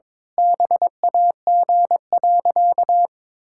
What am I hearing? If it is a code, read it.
Morse code, heard as a keyed tone, about 22 wpm, 700 Hz. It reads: BAG.
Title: Pangram Challenge v2 - 22wpm